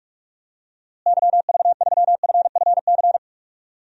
Morse code, heard as a keyed tone, about 46 wpm, 700 Hz. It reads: YV3FFC
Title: DXCC-100 Call Signs - 6 Characters - Repeated 3x - 45wpm